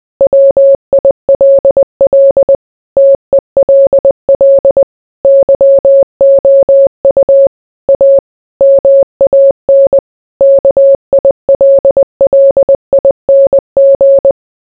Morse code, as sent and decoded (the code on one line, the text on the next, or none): .-- .. .-.. .-.. / - . .-.. .-.. / -.-- --- ..- / .- / -- .- -. / -.- .. .-.. .-.. .. -. --.
WILL TELL YOU A MAN KILLING